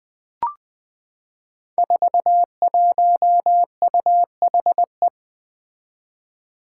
Morse code, heard as a keyed tone, about 20 wpm, 700 Hz. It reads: 41UHE
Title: Five Challenging Letter-Number Combinations 20wpm